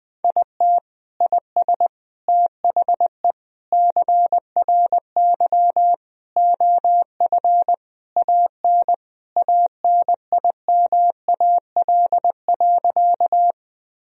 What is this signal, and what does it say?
Morse code, heard as a keyed tone, about 21 wpm, 700 Hz. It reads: IT IS THE CRY OF AN ANIMAL.
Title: Sentences from Top 1000 Words - Review Part 2 - 20wpm